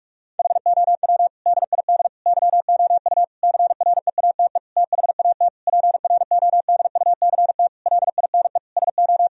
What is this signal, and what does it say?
Morse code, heard as a keyed tone, about 46 wpm, 700 Hz. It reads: HOW DID YOU CREATE THAT PRODUCT LINE SO
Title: Sentences from Top 1000 Words - Review Part 3 - 45wpm